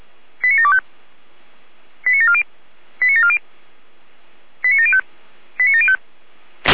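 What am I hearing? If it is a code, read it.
Five-tone selective call (ZVEI), heard as quick runs of five tones, tones of about 70 ms. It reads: 89725 89740 89740 89705 89705
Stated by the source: ZVEI five-tone call received on an SDR.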